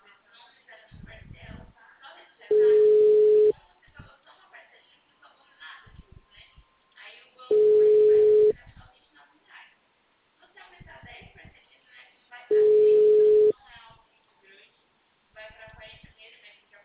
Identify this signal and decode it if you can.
Ringback tone (Europe, 425 Hz), with speech.
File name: out- -2004- - - .1157